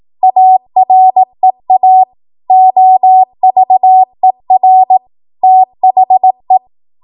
Morse code, heard as a keyed tone, about 18 wpm, 750 Hz. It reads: AREA OVER THE
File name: Morse / 18WPM